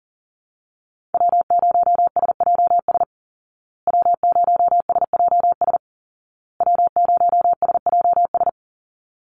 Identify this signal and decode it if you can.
Morse code, heard as a keyed tone, about 40 wpm, 700 Hz. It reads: W0SJS W0SJS W0SJS
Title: Call Signs Repeated 2x - 40wpm